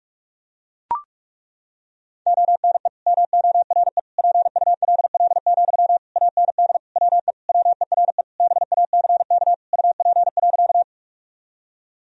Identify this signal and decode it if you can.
Morse code, heard as a keyed tone, about 41 wpm, 700 Hz. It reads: ONE MORE PULL, AND WE WERE BACK UP.
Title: Sentences from Top 1000 Words - Review Part 2 - 45wpm